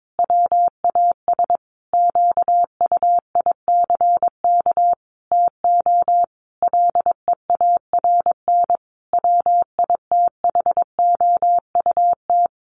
Morse code, heard as a keyed tone, about 22 wpm, 700 Hz. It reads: WAS QUICK TO LEARN WITHOUT
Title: Sentences from Top 1200 Words 22wpm